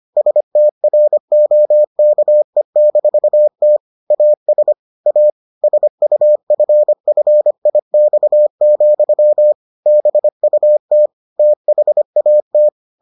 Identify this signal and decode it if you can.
Morse code, heard as a keyed tone, about 25 wpm, 600 Hz. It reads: STROKE-T AS A SUFFIX, BUT THAT